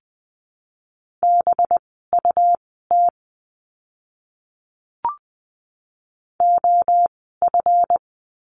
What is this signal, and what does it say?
Morse code, heard as a keyed tone, about 20 wpm, spaced as for 15 wpm, 700 Hz. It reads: BUT OF
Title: 206 Lesson 33f - Letter j - Words 20wpm